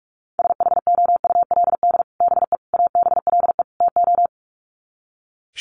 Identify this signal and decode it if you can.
Morse code, heard as a keyed tone, about 44 wpm, 700 Hz. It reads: SHOULD BE ABLE TO
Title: Sets of 4 Words 45wpm